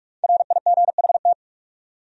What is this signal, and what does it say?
Morse code, heard as a keyed tone, about 59 wpm, 700 Hz. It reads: RIGHT